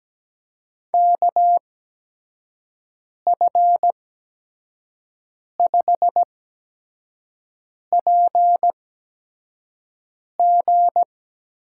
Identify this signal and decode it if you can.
Morse code, heard as a keyed tone, about 17 wpm, spaced as for 5 wpm, 700 Hz. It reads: KF5PG
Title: USA Call Signs - 6 Characters - 8x Character Spacing - 17wpm